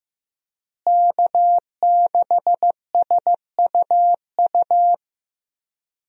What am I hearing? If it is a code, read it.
Morse code, heard as a keyed tone, about 15 wpm, 700 Hz. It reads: K6SUU
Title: USA Call Signs - 5 Characters - Repeated 3x - 15wpm